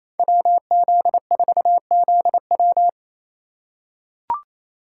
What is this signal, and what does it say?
Morse code, heard as a keyed tone, about 28 wpm, 700 Hz. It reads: WZ4ZW
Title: USA Call Signs - 5 Characters - 2x Character Spacing - 28wpm